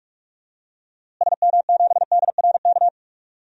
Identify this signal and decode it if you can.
Morse code, heard as a keyed tone, about 46 wpm, 700 Hz. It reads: SM7DRK
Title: DXCC-100 Call Signs - 6 Characters - Repeated 2x - 45wpm